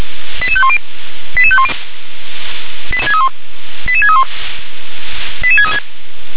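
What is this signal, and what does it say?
Five-tone selective call (ZVEI), heard as quick runs of five tones, tones of about 80 ms. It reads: 88510 88510 88631 88636
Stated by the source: ZVEI five-tone selective call received by SDR.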